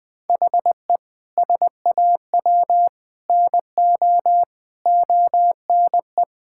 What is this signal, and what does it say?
Morse code, heard as a keyed tone, about 20 wpm, 700 Hz. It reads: HE SAW NO ONE
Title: Sentences from Top 1000 Words - Review Part 6 - 20wpm